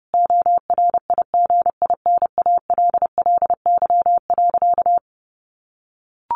Morse code, heard as a keyed tone, about 30 wpm, 700 Hz. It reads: ORIGINALLY.
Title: A Sound Of Thunder 30wpm